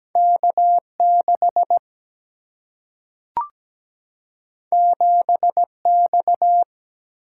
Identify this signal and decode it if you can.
Morse code, heard as a keyed tone, about 17 wpm, 700 Hz. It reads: K6 7X
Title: Single-Letter Single-Number 17wpm